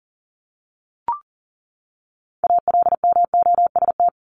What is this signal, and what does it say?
Morse code, heard as a keyed tone, about 40 wpm, 700 Hz. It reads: ALMOST
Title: Top 501-750 Words 40wpm